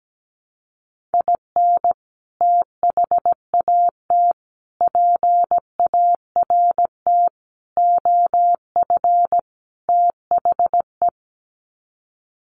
Morse code, heard as a keyed tone, about 17 wpm, 700 Hz. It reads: IN THAT PART OF THE
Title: Sets of 5 Words Encore2 17wpm